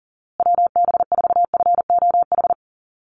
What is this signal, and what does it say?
Morse code, heard as a keyed tone, about 40 wpm, 700 Hz. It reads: WB4FOH
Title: USA Call Signs Repeated 3x 40wpm